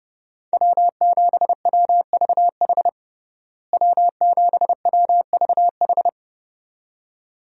Morse code, heard as a keyed tone, about 32 wpm, 700 Hz. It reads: W7WVH W7WVH
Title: USA Call Signs - 5 Characters - Repeated 2x - 30wpm